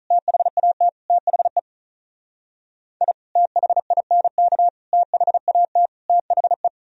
Morse code, heard as a keyed tone, about 44 wpm, 700 Hz. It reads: THAT THE I THINK THAT THE